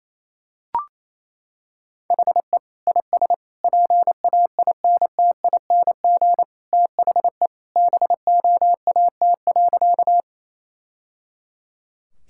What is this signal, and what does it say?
Morse code, heard as a keyed tone, about 28 wpm, 700 Hz. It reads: HE IS PAINTING THE BOAT.
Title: Sentences from Top 1000 Words - Review Part 6 - 28wpm